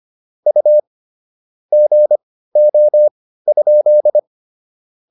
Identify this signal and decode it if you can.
Morse code, heard as a keyed tone, about 25 wpm, spaced as for 15 wpm, 600 Hz. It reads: U GO?